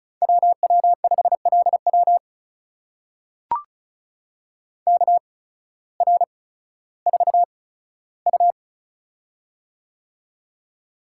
Morse code, heard as a keyed tone, about 36 wpm, spaced as for 16 wpm, 700 Hz. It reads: WW5LW KR4U
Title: USA Call Signs - 8x Character Spacing - 35wpm